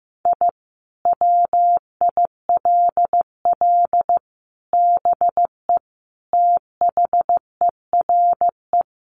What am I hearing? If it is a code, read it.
Morse code, heard as a keyed tone, about 14 wpm, 700 Hz. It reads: I WILL BE THERE
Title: Sentences from Top 1500 Words 15wpm